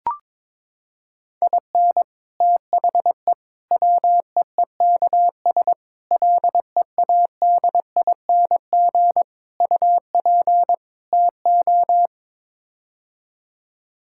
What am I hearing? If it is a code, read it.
Morse code, heard as a keyed tone, about 22 wpm, 700 Hz. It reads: IN THE WEEKS LEADING UP TO
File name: practice / 22wpm / Sets of 6 Words - 22wpm